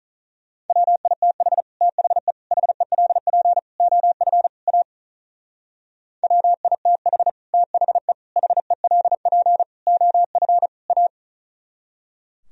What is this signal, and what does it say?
Morse code, heard as a keyed tone, about 44 wpm, 700 Hz. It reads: WITH THE HELP OF A WITH THE HELP OF A
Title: Speed-Racing - Sets of 5 Words 35wpm